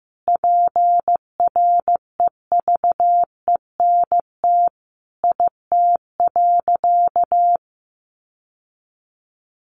Morse code, heard as a keyed tone, about 15 wpm, 700 Hz. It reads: PREVENT IT.
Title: Sentences from Top 1100 Words 15wpm